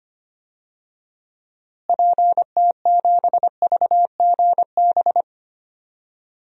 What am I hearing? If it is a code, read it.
Morse code, heard as a keyed tone, about 25 wpm, 700 Hz. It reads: PT7VGB
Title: DXCC-100 Call Signs - 6 Characters - Repeated 2x - 25wpm